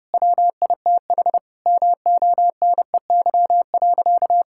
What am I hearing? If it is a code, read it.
Morse code, heard as a keyed tone, about 30 wpm, 700 Hz. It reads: WITH MONEY.
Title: Sentences from Top 1600 Words 30wpm